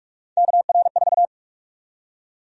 Morse code, heard as a keyed tone, about 47 wpm, 700 Hz. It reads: KR4
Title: USA Call Sign Prefixes - Repeated 2x - 45wpm